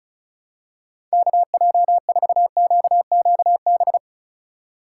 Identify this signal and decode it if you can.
Morse code, heard as a keyed tone, about 36 wpm, 700 Hz. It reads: KJ4QQB
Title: USA Call Signs - 6 Characters - 3x Character Spacing - 35wpm